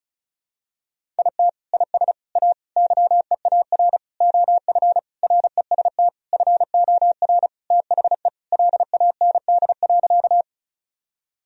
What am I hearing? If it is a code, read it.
Morse code, heard as a keyed tone, about 36 wpm, 700 Hz. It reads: IT IS A YEAR OF REST FOR THE LAND.
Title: Sentences from Top 1000 Words - Review Part 1 - 35wpm